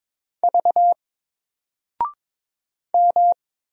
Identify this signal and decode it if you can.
Morse code, heard as a keyed tone, about 22 wpm, 700 Hz. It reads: V M
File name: Single-Letter-Number-22wpm